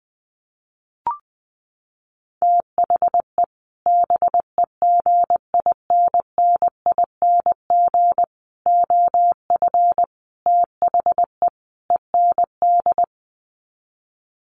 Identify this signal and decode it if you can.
Morse code, heard as a keyed tone, about 20 wpm, 700 Hz. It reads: THE BEGINNING OF THE END
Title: Sets of 5 Words Encore2 20wpm